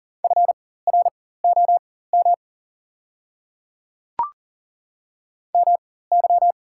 Morse code, heard as a keyed tone, about 43 wpm, spaced as for 19 wpm, 690 Hz. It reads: FROM MY
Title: 4x Character Spacing - Top 100 Words - 40wpm